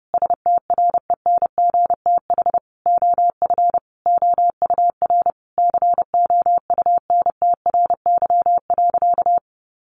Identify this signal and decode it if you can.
Morse code, heard as a keyed tone, about 30 wpm, 700 Hz. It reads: STRENGTH OF OUR COUNTRY.
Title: Sentences from Top 1300 Words 30wpm